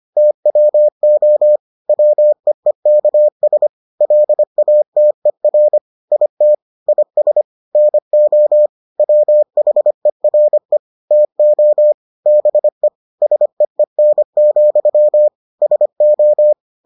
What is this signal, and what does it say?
Morse code, heard as a keyed tone, about 25 wpm, 600 Hz. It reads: TWO WEEKS LATER IT IS NO WHERE TO BE SEEN, SO